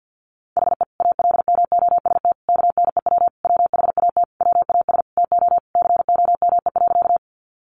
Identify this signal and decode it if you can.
Morse code, heard as a keyed tone, about 50 wpm, 700 Hz. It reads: HE ALMOST KNEW WHAT WAS TO COME.
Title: Sentences from Top 700 Words 50wpm